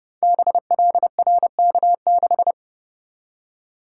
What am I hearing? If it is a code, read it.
Morse code, heard as a keyed tone, about 29 wpm, 700 Hz. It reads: BLRK6
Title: Code Groups - 5 Letter:Numbers - 30wpm